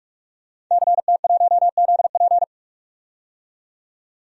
Morse code, heard as a keyed tone, about 44 wpm, 690 Hz. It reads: CT1ZP
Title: DXCC-100 Call Signs - 5 Characters - 45wpm